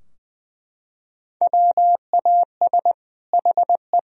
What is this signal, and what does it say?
Morse code, heard as a keyed tone, about 20 wpm, 700 Hz. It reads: WAS HE
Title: Sentences from Top 1000 Words - Review Part 4 - 20wpm